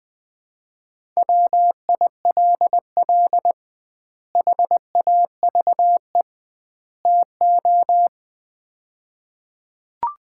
Morse code, heard as a keyed tone, about 20 wpm, 700 Hz. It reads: WILL HAVE TO
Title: 2x Word-Spacing - Sets of 3 Words - 20wpm